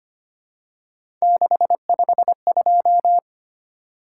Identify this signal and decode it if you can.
Morse code, heard as a keyed tone, about 25 wpm, 700 Hz. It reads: 652
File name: Three Letter-Number Combinations - 25wpm